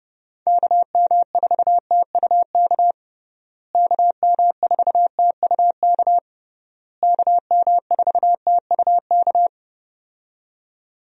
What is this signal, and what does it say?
Morse code, heard as a keyed tone, about 29 wpm, 700 Hz. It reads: KM4TUK KM4TUK KM4TUK
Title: USA Call Signs - 6 Characters - Repeated 3x - 30wpm